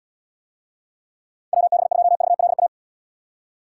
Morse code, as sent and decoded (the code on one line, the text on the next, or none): .-- -... ..--- ...- .-.. .-
WB2VLA